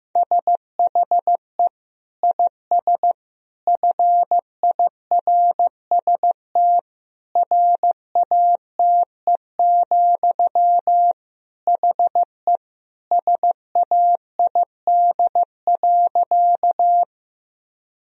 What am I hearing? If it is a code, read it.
Morse code, heard as a keyed tone, about 15 wpm, 700 Hz. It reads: SHE IS FIRST RATE, HE SAID.
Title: Sentences from Top 1000 Words - Review Part 4 - 15wpm